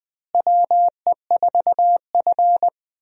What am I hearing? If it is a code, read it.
Morse code, heard as a keyed tone, about 20 wpm, 700 Hz. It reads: WE4F
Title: USA Call Signs - 4 Characters - Repeated 3x - 20wpm